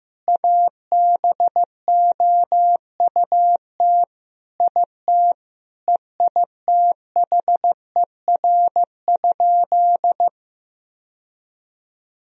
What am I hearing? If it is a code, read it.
Morse code, heard as a keyed tone, about 16 wpm, 700 Hz. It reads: ABOUT IT EITHER?
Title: Sentences from Top 1000 Words - Review Part 4 - 15wpm